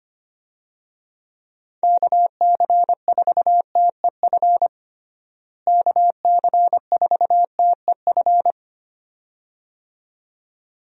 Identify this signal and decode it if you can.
Morse code, heard as a keyed tone, about 25 wpm, 700 Hz. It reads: KC4TEF KC4TEF
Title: USA Call Signs - 6 Characters - Repeated 2x - 25wpm